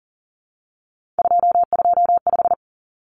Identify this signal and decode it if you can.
Morse code, heard as a keyed tone, about 40 wpm, 700 Hz. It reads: 225
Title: Common Cut Numbers - 3-Digit 40wpm